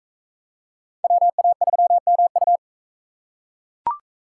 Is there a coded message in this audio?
Yes: Morse code from 1.0 to 2.6 s, heard as a keyed tone, about 39 wpm, 700 Hz. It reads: WA3MU